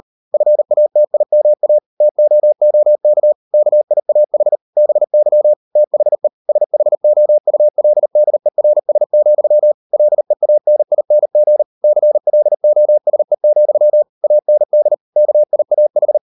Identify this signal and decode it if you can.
Morse code, heard as a keyed tone, about 38 wpm, 600 Hz. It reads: FATIMA TOOK KIAH BY THE SHOULDERS, LEANING CLOSE, AND KIAH